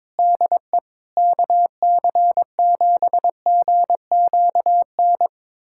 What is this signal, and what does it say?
Morse code, heard as a keyed tone, about 22 wpm, 700 Hz. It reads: DE KC7GQN
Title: USA Callsign DE Callsign 22wpm